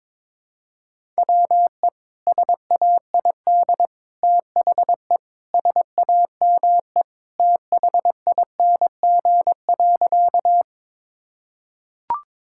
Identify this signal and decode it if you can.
Morse code, heard as a keyed tone, about 22 wpm, 700 Hz. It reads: WE SAID THE SAME THING.
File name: practice / 22wpm / Sentences with 3-Word Phrases - 22wpm